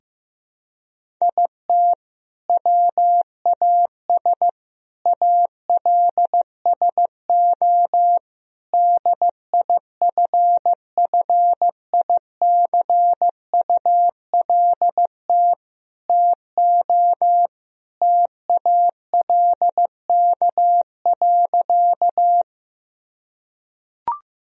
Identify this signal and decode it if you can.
Morse code, heard as a keyed tone, about 15 wpm, 700 Hz. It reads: IT WAS ALSO DIFFICULT TO TALK.
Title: Sentences from Top 1000 Words - Review Part 1 - 15wpm